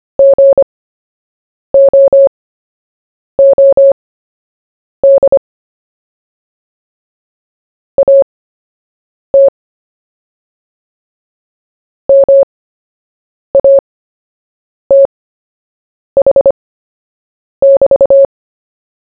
Morse code, heard as a keyed tone, about 25 wpm, spaced as for 7 wpm, 550 Hz. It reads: GOOD AT MATH<BT>
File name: cw / fists / Short-Story-1-7wpm